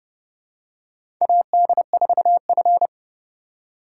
Morse code, heard as a keyed tone, about 30 wpm, 700 Hz. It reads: AD4F
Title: USA Call Signs - 4 Characters - 3x Character Spacing - 30wpm